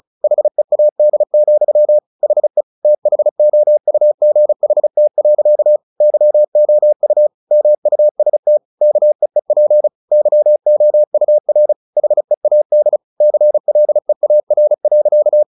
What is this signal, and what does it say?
Morse code, heard as a keyed tone, about 34 wpm, 600 Hz. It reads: HEAD, HE THOUGHT. YOU MUST KEEP YOUR HEAD CLEAR.
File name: the old man and the sea 2 35wpm